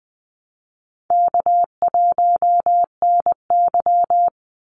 Morse code, heard as a keyed tone, about 20 wpm, 700 Hz. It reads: K1NY
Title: Easy Call Signs Repeated 3x - 20wpm